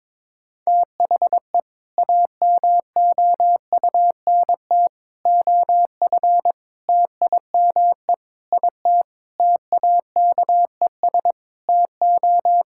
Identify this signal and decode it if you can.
Morse code, heard as a keyed tone, about 22 wpm, 700 Hz. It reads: THE AMOUNT OF TIME IT TAKES TO
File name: Sentences with 7-Word Phrases - 22wpm